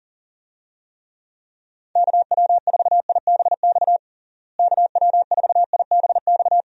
Morse code, heard as a keyed tone, about 40 wpm, 700 Hz. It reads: KW4IBX KW4IBX
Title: USA Call Signs - 6 Characters - Repeated 3x - 40wpm